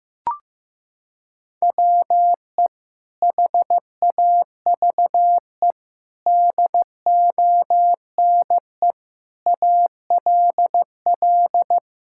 Morse code, heard as a keyed tone, about 15 wpm, 700 Hz. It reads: WE HAVE DONE ALL
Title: Sentences from Top 1000 Words - Review Part 5 - 15wpm